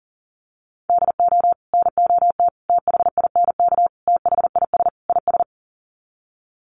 Morse code, heard as a keyed tone, about 37 wpm, 700 Hz. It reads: DO NOT THINK THIS IS
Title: Sets of 5 Words Encore2 40wpm